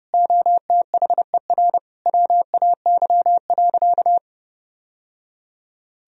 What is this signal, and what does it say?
Morse code, heard as a keyed tone, about 30 wpm, 700 Hz. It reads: OTHER WAY.
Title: Sentences from Top 1000 Words - Review Part 1 - 30wpm